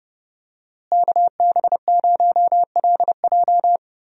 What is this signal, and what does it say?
Morse code, heard as a keyed tone, about 30 wpm, 700 Hz. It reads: KB0LJ
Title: USA Call Signs - 5 Characters - Repeated 2x - 30wpm